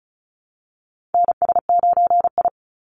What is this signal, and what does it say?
Morse code, heard as a keyed tone, about 34 wpm, 700 Hz. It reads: NS9I
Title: Call Signs Repeated 3x - 35wpm